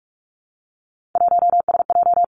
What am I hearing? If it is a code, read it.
Morse code, heard as a keyed tone, about 46 wpm, 700 Hz. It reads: 1SJ
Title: Three Challenging Letter-Number Combinations 45wpm